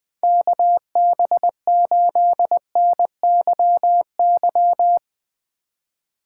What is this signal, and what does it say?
Morse code, heard as a keyed tone, about 21 wpm, 700 Hz. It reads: KB8NYY
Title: Speed Racing - USA 2x3 Call Signs 17wpm